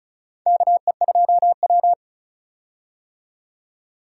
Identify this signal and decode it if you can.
Morse code, heard as a keyed tone, about 36 wpm, 700 Hz. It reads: KE2W
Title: USA Call Signs - 4 Characters - 35wpm